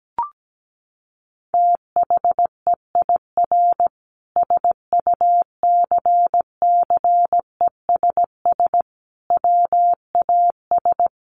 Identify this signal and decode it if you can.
Morse code, heard as a keyed tone, about 17 wpm, 700 Hz. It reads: THEIR SUCCESS WAS
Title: Sentences from Top 1500 Words 17wpm